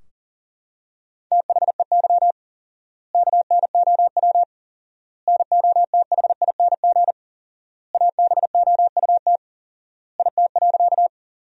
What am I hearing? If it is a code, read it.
Morse code, heard as a keyed tone, about 40 wpm, 700 Hz. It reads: THEY KNOW NOTHING ABOUT IT.
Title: 4x Word-Spacing - Sentences from Top 500 Words - 40wpm